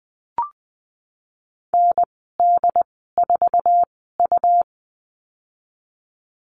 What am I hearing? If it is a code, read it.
Morse code, heard as a keyed tone, about 20 wpm, spaced as for 15 wpm, 700 Hz. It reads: ND4U
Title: 147 Lesson 24g - Letter y - Call Signs 20wpm